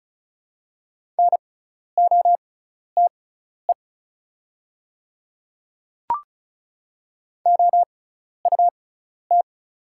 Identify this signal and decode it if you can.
Morse code, heard as a keyed tone, about 34 wpm, spaced as for 12 wpm, 690 Hz. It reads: NOTE OUT
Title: 6x Character Spacing - Top 500 Words - 35wpm